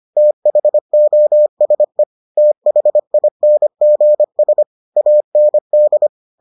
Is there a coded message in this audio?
Yes: Morse code from 0.2 to 6.1 s, heard as a keyed tone, about 25 wpm, 600 Hz. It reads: THOSE THINGS AND